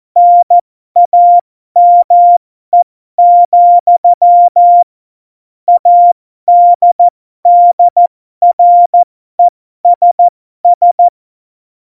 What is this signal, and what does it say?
Morse code, heard as a keyed tone, about 14 wpm, spaced as for 12 wpm, 700 Hz. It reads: NAME, ADDRESS
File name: Morsekurs / Witze